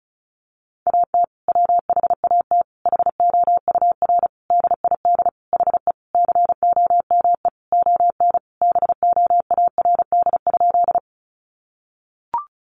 Morse code, heard as a keyed tone, about 34 wpm, 700 Hz. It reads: AT WHAT HOUR DID HE COME ON BOARD?
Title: Sentences from Top 700 Words 35wpm